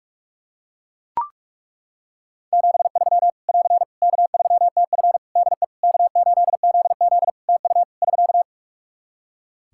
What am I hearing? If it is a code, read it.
Morse code, heard as a keyed tone, about 41 wpm, 700 Hz. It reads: 73 <AR> K3TF DE K8ZZ TU <SK>
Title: K1USN SST QSO Practice 45wpm